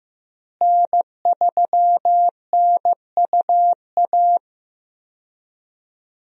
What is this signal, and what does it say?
Morse code, heard as a keyed tone, about 15 wpm, 700 Hz. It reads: N3NUA
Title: USA Call Signs - 5 Characters - Repeated 3x - 15wpm